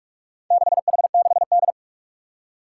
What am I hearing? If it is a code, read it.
Morse code, heard as a keyed tone, about 44 wpm, 700 Hz. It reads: 6H6D
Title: Four Challenging Letter-Number Combinations 45wpm